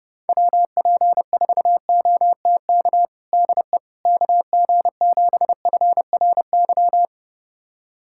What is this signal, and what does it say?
Morse code, heard as a keyed tone, about 30 wpm, 700 Hz. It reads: WP4OTK DE KG7FRY
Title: USA Callsign DE Callsign - 6 Characters - 30wpm